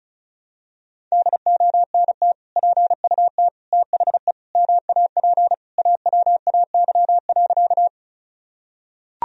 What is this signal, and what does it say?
Morse code, heard as a keyed tone, about 36 wpm, 700 Hz. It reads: DONT PUT THE MAP AWAY.
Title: Sentences from Top 1000 Words - Review Part 2 - 35wpm